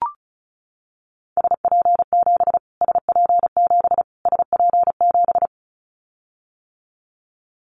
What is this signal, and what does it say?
Morse code, heard as a keyed tone, about 34 wpm, 700 Hz. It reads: SP7 SP7 SP7
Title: Callsign Prefixes for Top 25 DXCC - Repeated 3x - 35wpm